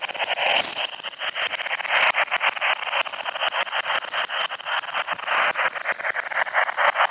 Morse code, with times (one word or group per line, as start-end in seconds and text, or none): none